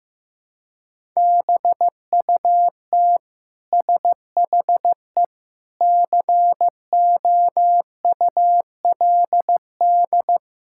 Morse code, heard as a keyed tone, about 15 wpm, 700 Hz. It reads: BUT SHE COULD
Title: Sentences from Top 1000 Words - Review Part 6 - 15wpm